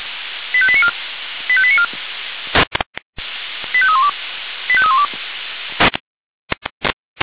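Five-tone selective call (ZVEI), heard as quick runs of five tones, tones of about 70 ms. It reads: 85894 85894 85312 85312
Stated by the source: ZVEI five-tone selective call 85310, received with an SDR.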